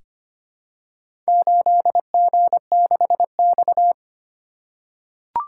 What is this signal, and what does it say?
Morse code, heard as a keyed tone, about 25 wpm, 700 Hz. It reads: 8G6X
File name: practice / 25wpm / Four Letter-Number Combinations - 25wpm